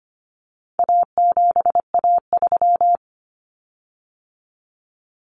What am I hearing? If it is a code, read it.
Morse code, heard as a keyed tone, about 25 wpm, 700 Hz. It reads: A7A3
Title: Common Cut Numbers - 4-Digit 25wpm